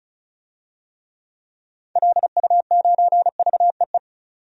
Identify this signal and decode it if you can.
Morse code, heard as a keyed tone, about 34 wpm, 700 Hz. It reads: LU9VEE
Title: DXCC-100 Call Signs - 6 Characters - 35wpm